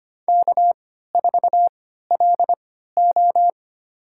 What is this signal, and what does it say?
Morse code, heard as a keyed tone, about 25 wpm, spaced as for 14 wpm, 700 Hz. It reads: K4LO
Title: USA Call Signs - 4 Characters - 3x Character Spacing - 25wpm